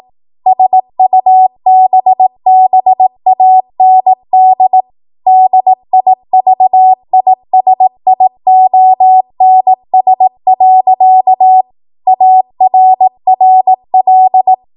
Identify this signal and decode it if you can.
Morse code, heard as a keyed tone, about 18 wpm, 750 Hz. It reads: SUBBAND DIVISIONS. ARRL
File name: Morse / 18WPM